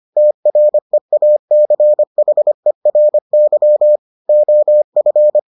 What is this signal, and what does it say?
Morse code, heard as a keyed tone, about 25 wpm, 600 Hz. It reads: TREACHERY OF